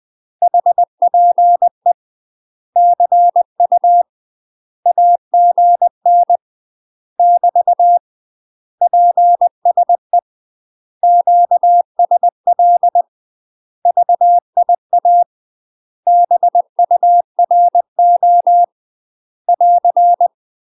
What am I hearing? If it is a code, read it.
Morse code, heard as a keyed tone, about 20 wpm, 700 Hz. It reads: HPE CU AGN <BT> PSE QSL VIA BURO <AR>
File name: UK MASTER QSOs TRAINING / 20WPM / UK MASTER QSOs 0010